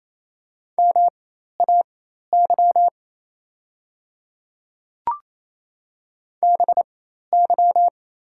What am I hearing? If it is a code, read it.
Morse code, heard as a keyed tone, about 28 wpm, spaced as for 13 wpm, 700 Hz. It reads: MAY BY
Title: 4x Character Spacing - Top 200 Words - 28wpm